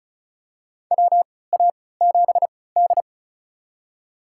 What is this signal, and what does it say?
Morse code, heard as a keyed tone, about 36 wpm, spaced as for 20 wpm, 700 Hz. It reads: WA7D
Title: USA Call Signs - 3x Character Spacing - 35wpm